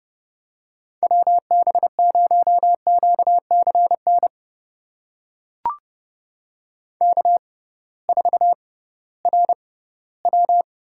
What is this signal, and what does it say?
Morse code, heard as a keyed tone, about 30 wpm, spaced as for 17 wpm, 700 Hz. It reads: WB0QCN K4RW
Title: USA Call Signs - 6x Character Spacing - 30wpm